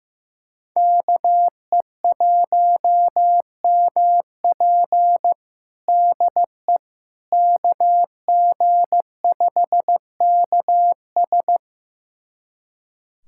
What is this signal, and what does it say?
Morse code, heard as a keyed tone, about 15 wpm, 700 Hz. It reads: KE1MP DE KG5KS
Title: USA Callsign DE Callsign - 5 Characters - 15wpm